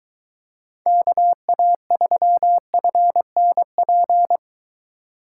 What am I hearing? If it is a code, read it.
Morse code, heard as a keyed tone, about 23 wpm, 700 Hz. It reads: KA3FNP